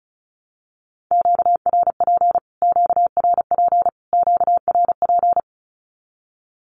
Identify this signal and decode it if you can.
Morse code, heard as a keyed tone, about 34 wpm, 700 Hz. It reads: QRP QRP QRP
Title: Q-Codes Repeated 3x 35wpm